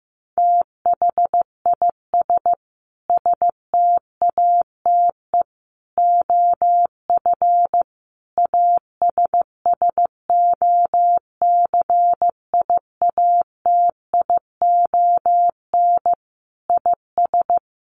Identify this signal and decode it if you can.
Morse code, heard as a keyed tone, about 15 wpm, 700 Hz. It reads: THIS STATE OF ASSOCIATION IS
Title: Sentences from Top 1100 Words 15wpm